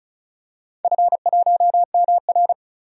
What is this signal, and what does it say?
Morse code, heard as a keyed tone, about 36 wpm, 700 Hz. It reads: F1MR
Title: DXCC-100 Call Signs - 4 Characters - 35wpm